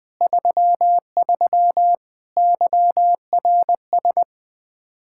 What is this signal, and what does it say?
Morse code, heard as a keyed tone, about 21 wpm, 700 Hz. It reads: 33 YRS
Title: Practice QSO - Exchange 3 - More Info - 20wpm